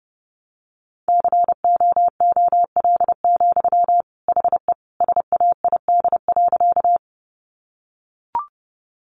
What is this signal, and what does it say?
Morse code, heard as a keyed tone, about 30 wpm, 700 Hz. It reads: COOL, HE SAID.
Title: Sentences from Top 1200 Words 30wpm